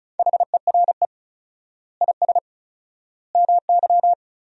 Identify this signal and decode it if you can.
Morse code, heard as a keyed tone, about 34 wpm, 700 Hz. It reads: HERE IS MY
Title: 4x Word-Spacing - Sentences from Top 200 Words - 35wpm